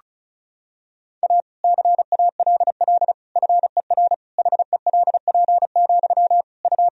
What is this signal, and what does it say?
Morse code, heard as a keyed tone, about 36 wpm, 700 Hz. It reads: A CALL FER HELP, U
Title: Sentences from Top 300 Words and Abbreviations 35wpm